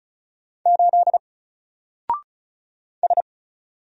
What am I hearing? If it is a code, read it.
Morse code, heard as a keyed tone, about 34 wpm, 700 Hz. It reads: 8 S
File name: Single-Letter-Number-35wpm